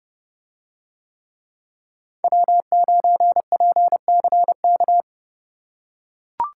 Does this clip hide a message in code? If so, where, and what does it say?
Yes: Morse code from 2.2 to 5.0 s, heard as a keyed tone, about 29 wpm, 700 Hz. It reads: W9PCK